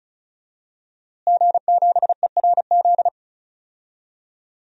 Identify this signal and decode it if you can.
Morse code, heard as a keyed tone, about 34 wpm, 700 Hz. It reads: G7ERZ
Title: DXCC-100 Call Signs - 5 Characters - Repeated 3x - 35wpm